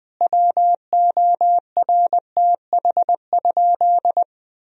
Morse code, heard as a keyed tone, about 20 wpm, 700 Hz. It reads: WORTH?
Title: Sentences from Top 1000 Words - Review Part 4 - 20wpm